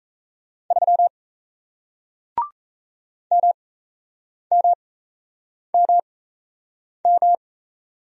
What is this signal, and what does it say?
Morse code, heard as a keyed tone, about 38 wpm, spaced as for 8 wpm, 700 Hz. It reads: 3 MMMM